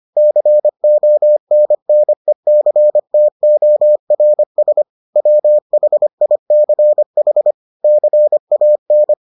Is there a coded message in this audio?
Yes: Morse code from 0.2 to 9.1 s, heard as a keyed tone, about 25 wpm, 600 Hz. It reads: CONNECTORS WHICH CAN